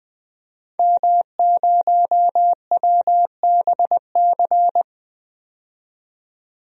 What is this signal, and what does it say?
Morse code, heard as a keyed tone, about 20 wpm, 700 Hz. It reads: M0WBC
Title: DXCC-100 Call Signs - 5 Characters - 20wpm